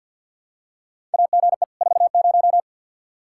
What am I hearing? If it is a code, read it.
Morse code, heard as a keyed tone, about 48 wpm, 700 Hz. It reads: AGE 40
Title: K1USN SST QSO Practice 50wpm